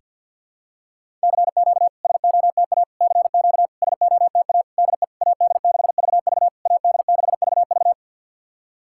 Morse code, heard as a keyed tone, about 53 wpm, 690 Hz. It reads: CQ SOTA CQ SOTA DE AD6VV AD6VV
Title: SOTA QSO Practice 50wpm